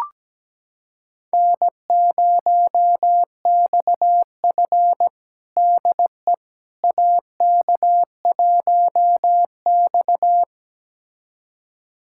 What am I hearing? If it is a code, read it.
Morse code, heard as a keyed tone, about 17 wpm, 700 Hz. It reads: N0XF DE AK1X
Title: USA Callsign DE Callsign - 4 Characters - 17wpm